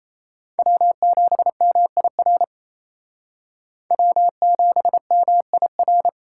Morse code, heard as a keyed tone, about 37 wpm, 700 Hz. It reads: W7MIR W7MIR